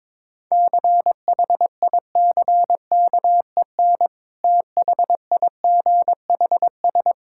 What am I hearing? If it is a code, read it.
Morse code, heard as a keyed tone, about 22 wpm, 690 Hz. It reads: CHICKEN THIGHS
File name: Difficult Words with Hints - 22wpm